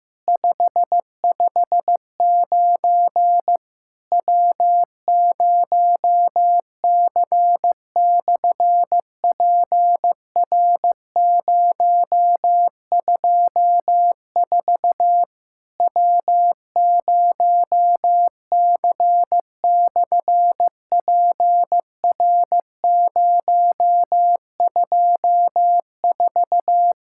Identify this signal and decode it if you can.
Morse code, heard as a keyed tone, about 15 wpm, 700 Hz. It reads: 559 W0C/PR024 W0C/PR024
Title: SOTA QSO Practice 15wpm